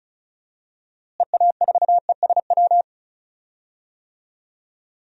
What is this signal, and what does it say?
Morse code, heard as a keyed tone, about 34 wpm, 700 Hz. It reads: EA4ESW
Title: DXCC-100 Call Signs - 6 Characters - Repeated 2x - 35wpm